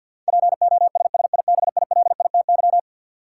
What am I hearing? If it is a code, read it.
Morse code, heard as a keyed tone, about 51 wpm, 700 Hz. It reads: POSSIBILITY
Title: Sets of 3 Words - New Word Drill - 50wpm